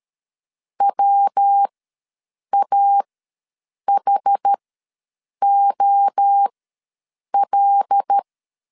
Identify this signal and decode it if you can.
Morse code, heard as a keyed tone, about 13 wpm, spaced as for 7 wpm, 800 Hz. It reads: WAHOL
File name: LEZIONI / 5 LBVC / 5 L ws 12wpm 2958